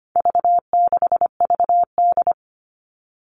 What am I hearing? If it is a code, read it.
Morse code, heard as a keyed tone, about 25 wpm, 700 Hz. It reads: V6VD
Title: All Cut Numbers - 4-Digit 25wpm